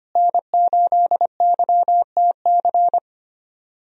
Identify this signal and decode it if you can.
Morse code, heard as a keyed tone, about 25 wpm, 690 Hz. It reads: N8YTC
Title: USA Call Signs - 5 Characters - Repeated 2x - 25wpm